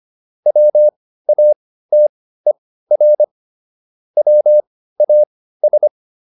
Morse code, heard as a keyed tone, about 25 wpm, spaced as for 15 wpm, 600 Hz. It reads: WATER WAS